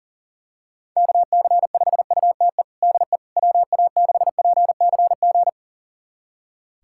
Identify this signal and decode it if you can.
Morse code, heard as a keyed tone, about 37 wpm, 700 Hz. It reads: KC5UTE DE WA6PCG
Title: USA Callsign DE Callsign 40wpm